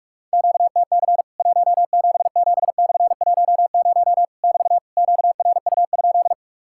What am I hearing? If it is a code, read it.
Morse code, heard as a keyed tone, about 44 wpm, 700 Hz. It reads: QTC 177/10 <BT> QRU?
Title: Worked All Europe DX Contest - QTC Exchange - 45wpm